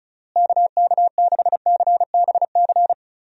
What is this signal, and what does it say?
Morse code, heard as a keyed tone, about 34 wpm, 700 Hz. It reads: KK6CBC
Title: USA Call Signs - 6 Characters - Repeated 3x - 35wpm